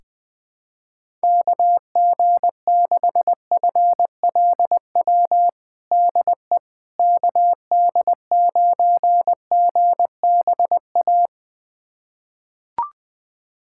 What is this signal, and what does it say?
Morse code, heard as a keyed tone, about 20 wpm, 700 Hz. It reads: KG6FLW DE KD9GBA
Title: USA Callsign DE Callsign 20wpm